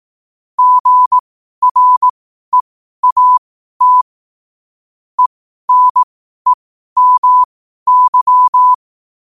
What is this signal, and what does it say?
Morse code, heard as a keyed tone, about 18 wpm, spaced as for 12 wpm, 1000 Hz. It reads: GREAT ENEMY
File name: qotdfiles / 12wpm